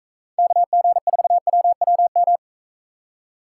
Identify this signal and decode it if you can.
Morse code, heard as a keyed tone, about 40 wpm, 700 Hz. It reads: KG4WWM